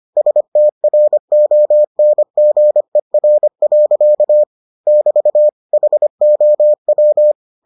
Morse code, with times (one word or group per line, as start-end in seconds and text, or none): 0.2-4.4 STRONGER.
4.9-5.5 <BT>
5.7-7.3 HOW